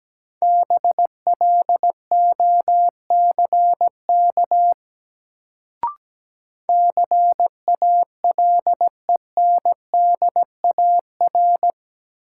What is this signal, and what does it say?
Morse code, heard as a keyed tone, about 17 wpm, 700 Hz. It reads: BLOCK CALENDAR
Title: Sentences from Top 1600 Words - New Word Drill - 17wpm